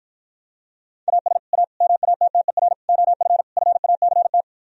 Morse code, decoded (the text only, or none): AS A MATTER OF FACT